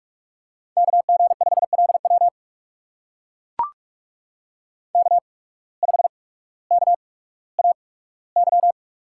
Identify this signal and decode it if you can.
Morse code, heard as a keyed tone, about 44 wpm, spaced as for 19 wpm, 700 Hz. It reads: KG5LW K5KAY
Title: USA Call Signs - 5 Characters - 8x Character Spacing - 45wpm